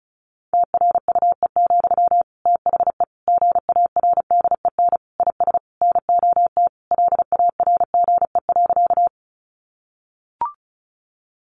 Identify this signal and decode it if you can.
Morse code, heard as a keyed tone, about 34 wpm, 700 Hz. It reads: TRUE, THE GARDEN IS NOT LARGE.
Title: Sentences from Top 700 Words 35wpm